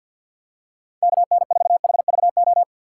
Morse code, heard as a keyed tone, about 52 wpm, 700 Hz. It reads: KN4HVO
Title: USA Call Signs - 6 Characters - 4x Character Spacing - 50wpm